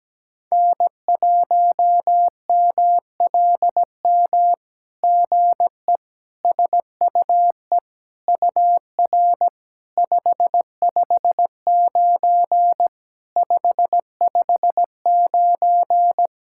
Morse code, heard as a keyed tone, about 17 wpm, 700 Hz. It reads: N1MLM GE SUE UR 559 559
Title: SOTA QSO Practice 17wpm